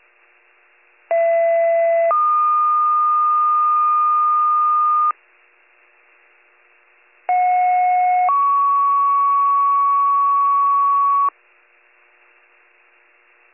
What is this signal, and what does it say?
Two-tone page, heard as a one-second tone, then a three-second tone, about 670 then 1220 Hz, about 730 then 1120 Hz. It is heard from 1.1 to 11.3 s.